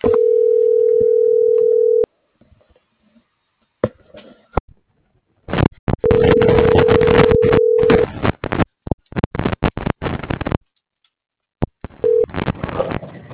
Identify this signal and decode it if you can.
Ringback tone (North America), 440 + 480 Hz.